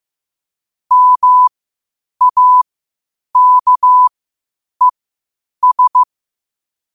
Morse code, heard as a keyed tone, about 15 wpm, spaced as for 8 wpm, 1000 Hz. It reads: MAKES
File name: qotdfiles / 8wpm